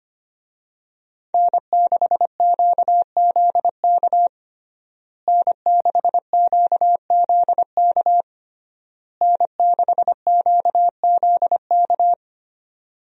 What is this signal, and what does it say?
Morse code, heard as a keyed tone, about 25 wpm, 700 Hz. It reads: N6QZK N6QZK N6QZK
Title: USA Call Signs - 5 Characters - Repeated 3x - 25wpm